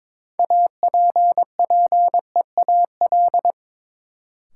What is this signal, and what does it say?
Morse code, heard as a keyed tone, about 22 wpm, 700 Hz. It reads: APPEAL
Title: Sentences from Top 1300 Words - New Word Drill - 22wpm